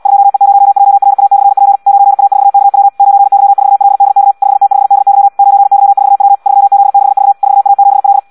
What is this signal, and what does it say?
Morse code, heard as a keyed tone, about 74 wpm, 800 Hz. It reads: 2E0JRA/P 9A4KW 9Y4LDK 5T5DY 9Y4W 3Z6V 4A7L
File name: Audio practica CW